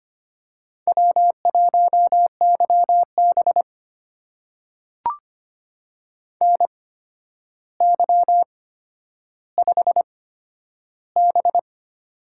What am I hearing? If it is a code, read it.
Morse code, heard as a keyed tone, about 25 wpm, spaced as for 11 wpm, 700 Hz. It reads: W1YB NY5B
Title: USA Call Signs - 4 Characters - 8x Character Spacing - 25wpm